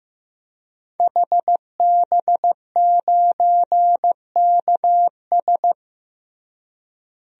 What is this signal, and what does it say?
Morse code, heard as a keyed tone, about 15 wpm, 700 Hz. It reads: HB9KS
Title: DXCC-100 Call Signs - 5 Characters - Repeated 2x - 15wpm